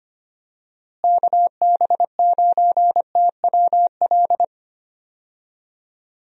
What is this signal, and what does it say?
Morse code, heard as a keyed tone, about 25 wpm, 700 Hz. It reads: KB9TWL
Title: USA Call Signs - 6 Characters - Repeated 2x - 25wpm